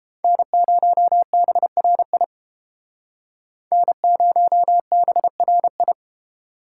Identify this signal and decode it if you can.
Morse code, heard as a keyed tone, about 30 wpm, 700 Hz. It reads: N0BRI N0BRI